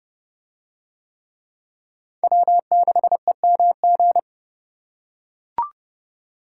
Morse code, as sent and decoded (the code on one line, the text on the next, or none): .-- -.... -- --.
W6EMG